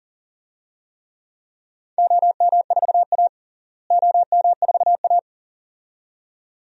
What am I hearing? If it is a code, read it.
Morse code, heard as a keyed tone, about 40 wpm, 700 Hz. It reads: OM4A OM4A
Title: DXCC-100 Call Signs - 4 Characters - Repeated 2x - 40wpm